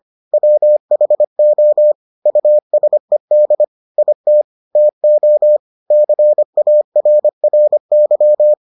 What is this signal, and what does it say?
Morse code, heard as a keyed tone, about 25 wpm, 600 Hz. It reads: WHO USED IT TO CARRY